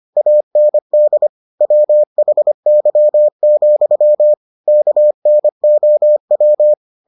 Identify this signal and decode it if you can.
Morse code, heard as a keyed tone, about 25 wpm, 600 Hz. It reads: AND WHY, KNOW